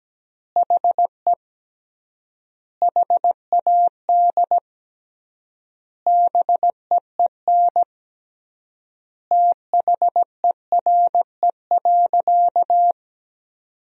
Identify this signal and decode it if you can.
Morse code, heard as a keyed tone, about 17 wpm, 700 Hz. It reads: HE HAD BEEN THERE.
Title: 3x Word-Spacing - Sentences from Top 300 Words - 17wpm